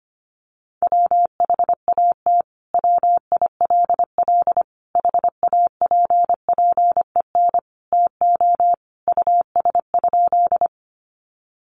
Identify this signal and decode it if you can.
Morse code, heard as a keyed tone, about 25 wpm, 700 Hz. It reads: WHAT WILL HAPPEN TO US?
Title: Sentences from Top 400 Words 25wpm